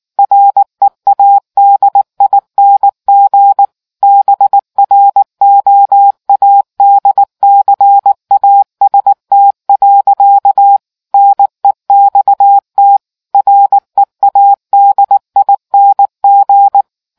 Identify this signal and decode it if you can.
Morse code, heard as a keyed tone, about 19 wpm, 800 Hz. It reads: READING BROADCAST. NEXT READING